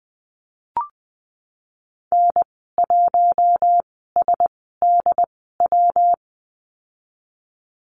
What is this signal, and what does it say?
Morse code, heard as a keyed tone, about 20 wpm, spaced as for 15 wpm, 700 Hz. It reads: N1SDW
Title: 122 Lesson 20g - Number 3 - Call Signs 20wpm